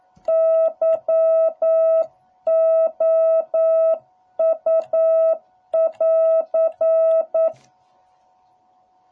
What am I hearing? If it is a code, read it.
Morse code, heard as a keyed tone, about 9 wpm, 660 Hz. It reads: YOU <AR>